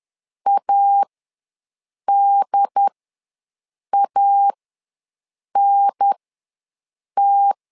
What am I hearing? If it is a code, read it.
Morse code, heard as a keyed tone, about 10 wpm, spaced as for 6 wpm, 800 Hz. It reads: ADANT
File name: 2 UHDN ws 10wpm